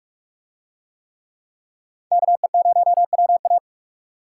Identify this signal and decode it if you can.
Morse code, heard as a keyed tone, about 44 wpm, 700 Hz. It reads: KE0WA